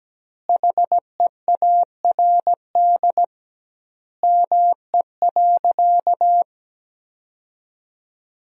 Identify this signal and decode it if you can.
Morse code, heard as a keyed tone, about 17 wpm, 700 Hz. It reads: HEARD ME.
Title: 2x Word-Spacing - Sentences from Top 400 Words - 17wpm